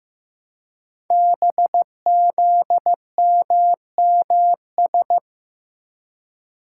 Morse code, heard as a keyed tone, about 15 wpm, 700 Hz. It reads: BZMMS
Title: Code Groups - 5 Letters - 15wpm